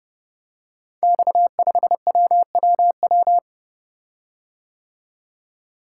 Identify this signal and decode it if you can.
Morse code, heard as a keyed tone, about 30 wpm, 700 Hz. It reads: X5WWW
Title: Code Groups - 5 Letter:Numbers - 30wpm